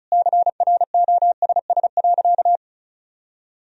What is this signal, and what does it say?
Morse code, heard as a keyed tone, about 36 wpm, 700 Hz. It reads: CROSS.
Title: Sentences from Top 1000 Words - Review Part 1 - 35wpm